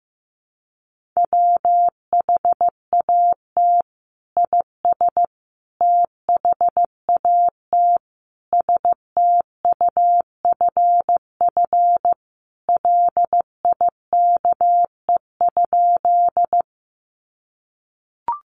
Morse code, heard as a keyed tone, about 15 wpm, 700 Hz. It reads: WHAT IS THAT STUFF LIKE?
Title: Sentences from Top 1400 Words 15wpm